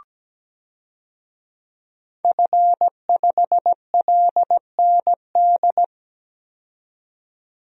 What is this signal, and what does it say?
Morse code, heard as a keyed tone, about 17 wpm, 700 Hz. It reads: F5LND
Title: DXCC-100 Call Signs - 5 Characters - 17wpm